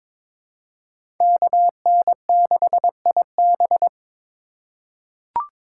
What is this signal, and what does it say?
Morse code, heard as a keyed tone, about 22 wpm, 700 Hz. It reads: KN6IB
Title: USA Call Signs - 5 Characters - 4x Character Spacing - 22wpm